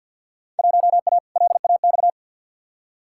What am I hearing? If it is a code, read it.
Morse code, heard as a keyed tone, about 53 wpm, 700 Hz. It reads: 1A LAX
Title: Field Day Exchanges - 50wpm